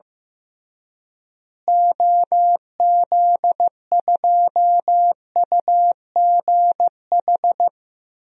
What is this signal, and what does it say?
Morse code, heard as a keyed tone, about 15 wpm, 700 Hz. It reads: OZ2UGH
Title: DXCC-100 Call Signs - 6 Characters - Repeated 3x - 15wpm